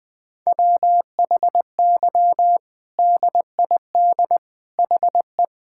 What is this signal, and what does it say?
Morse code, heard as a keyed tone, about 20 wpm, 700 Hz. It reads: WHY DID HE
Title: Sentences from Top 1000 Words - Review Part 5 - 20wpm